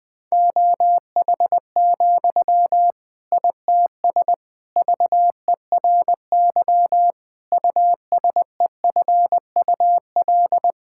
Morse code, heard as a keyed tone, about 20 wpm, 700 Hz. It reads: OH, ITS VERY USEFUL
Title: Sentences from Top 1000 Words - Review Part 6 - 20wpm